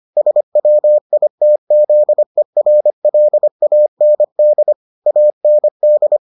Morse code, heard as a keyed tone, about 25 wpm, 600 Hz. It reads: SWITZERLAND AND